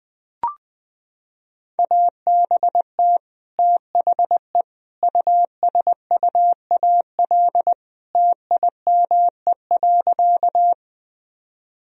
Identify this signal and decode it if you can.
Morse code, heard as a keyed tone, about 21 wpm, 700 Hz. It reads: ABT THE USUAL TIME.
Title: Sentences from Top 400 Words and Abbreviations 20wpm